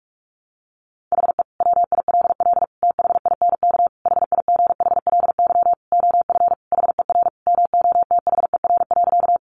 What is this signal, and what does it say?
Morse code, heard as a keyed tone, about 44 wpm, 700 Hz. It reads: HE WILL THINK HIGHLY OF HER MOTHER.
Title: Sentences from Top 1100 Words 45wpm